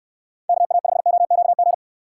Morse code, heard as a keyed tone, about 63 wpm, 700 Hz. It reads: KA5OQG